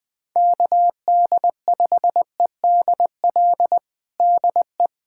Morse code, heard as a keyed tone, about 20 wpm, 700 Hz. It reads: KD5EDL DE
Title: POTA QSO Practice - 20wpm